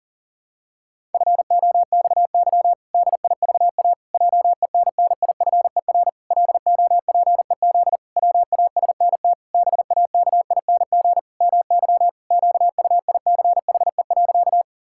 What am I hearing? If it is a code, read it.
Morse code, heard as a keyed tone, about 40 wpm, 700 Hz. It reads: FOXY DIVA JENNIFER LOPEZ WASNT BAKING MY QUICHE.
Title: Pangram Challenge v2 - 40wpm